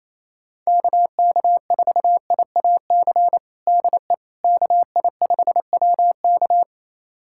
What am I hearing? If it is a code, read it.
Morse code, heard as a keyed tone, about 28 wpm, 700 Hz. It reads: KK4IAC DE KI5WK
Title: USA Callsign DE Callsign 28wpm